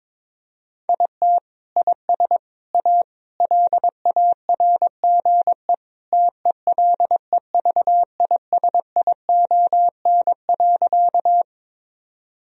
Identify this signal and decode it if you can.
Morse code, heard as a keyed tone, about 22 wpm, 700 Hz. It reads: IT IS A LARGE TELEVISION.
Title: Sentences from Top 1000 Words - Review Part 4 - 22wpm